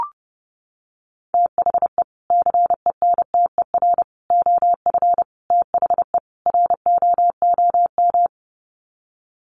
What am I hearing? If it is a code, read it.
Morse code, heard as a keyed tone, about 30 wpm, 700 Hz. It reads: THE CENTER OF THE ROOM
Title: Sets of 5 Words Encore 30wpm